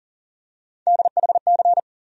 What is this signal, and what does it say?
Morse code, heard as a keyed tone, about 38 wpm, 700 Hz. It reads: DHC
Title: USA Call Sign Postfixes - 3 Letters 40wpm